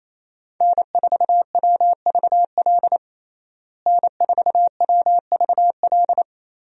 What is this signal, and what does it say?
Morse code, heard as a keyed tone, about 28 wpm, 700 Hz. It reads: N4WVL N4WVL
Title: USA Call Signs - 5 Characters - Repeated 2x - 28wpm